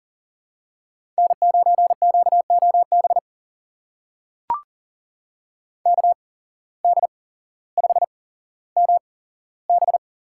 Morse code, heard as a keyed tone, about 40 wpm, spaced as for 17 wpm, 700 Hz. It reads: N9QOB KD5MB
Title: USA Call Signs - 5 Characters - 8x Character Spacing - 40wpm